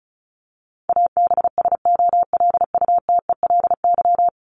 Morse code, heard as a keyed tone, about 34 wpm, 700 Hz. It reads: ABSOLUTELY
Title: LY Suffix - Top 100 Words with Base Words 35wpm